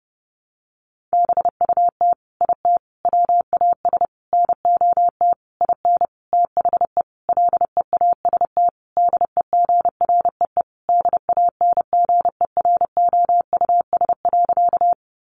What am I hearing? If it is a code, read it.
Morse code, heard as a keyed tone, about 30 wpm, 700 Hz. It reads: BUT IT WAS NOT IN THE LEAST DEGREE DANGEROUS.
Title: Sentences from Top 1500 Words 30wpm